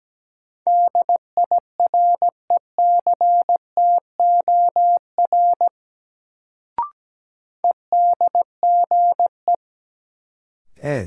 Morse code, heard as a keyed tone, about 17 wpm, 700 Hz. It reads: DIRECTOR EDGE
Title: Sets of 3 Words - New Word Drill - 17wpm